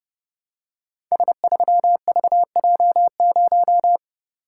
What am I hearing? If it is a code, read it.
Morse code, heard as a keyed tone, about 30 wpm, 700 Hz. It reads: S3VJ0
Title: Code Groups - 5 Letter:Numbers - 30wpm